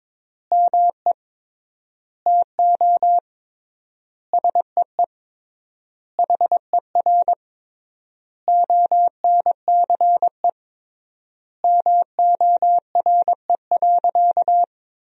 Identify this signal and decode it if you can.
Morse code, heard as a keyed tone, about 21 wpm, 700 Hz. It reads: ME TO SEE HER ONCE MORE.
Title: 3x Word-Spacing - Sentences from Top 500 Words - 22wpm